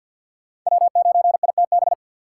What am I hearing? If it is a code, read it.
Morse code, heard as a keyed tone, about 48 wpm, 700 Hz. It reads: W9ITB
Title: USA Call Signs - 5 Characters - 6x Character Spacing - 50wpm